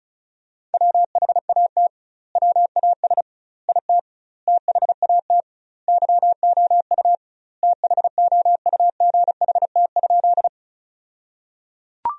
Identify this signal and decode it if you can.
Morse code, heard as a keyed tone, about 36 wpm, 700 Hz. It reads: WHAT WAS IT THAT YOU THOUGHT?
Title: 2x Word-Spacing - Sentences from Top 300 Words - 35wpm